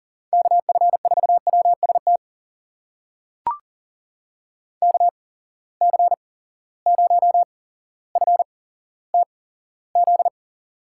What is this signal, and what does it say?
Morse code, heard as a keyed tone, about 40 wpm, spaced as for 17 wpm, 700 Hz. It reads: KF4WST KC0FTZ
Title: USA Call Signs - 6 Characters - 8x Character Spacing - 40wpm